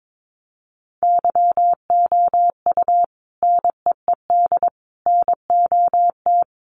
Morse code, heard as a keyed tone, about 22 wpm, 700 Hz. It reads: YOU NEED NOT
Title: Sentences from Top 1500 Words 22wpm